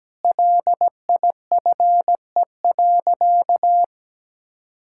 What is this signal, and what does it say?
Morse code, heard as a keyed tone, about 17 wpm, 700 Hz. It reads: LIFE.
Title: Sentences from Top 1000 Words - Review Part 5 - 17wpm